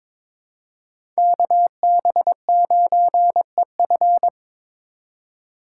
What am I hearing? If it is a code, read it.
Morse code, heard as a keyed tone, about 22 wpm, 700 Hz. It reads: KB9EF
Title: USA Call Signs - 2x Character Spacing - 22wpm